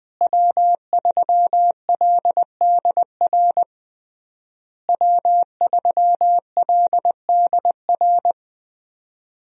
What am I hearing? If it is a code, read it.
Morse code, heard as a keyed tone, about 21 wpm, 700 Hz. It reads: W3LDR W3LDR
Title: USA Call Signs - 5 Characters - Repeated 3x - 20wpm